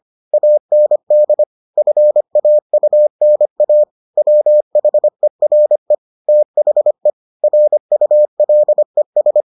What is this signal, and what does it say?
Morse code, heard as a keyed tone, about 25 wpm, 600 Hz. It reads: AND FAUNA WHERE THE RULES